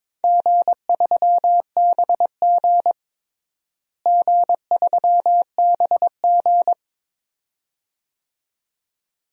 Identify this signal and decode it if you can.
Morse code, heard as a keyed tone, about 22 wpm, 700 Hz. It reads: G3BG G3BG
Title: DXCC-100 Call Signs - 4 Characters - Repeated 2x - 22wpm